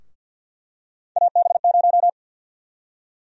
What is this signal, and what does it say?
Morse code, heard as a keyed tone, about 46 wpm, 700 Hz. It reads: AB0
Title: USA Call Sign Prefixes - Repeated 2x - 50wpm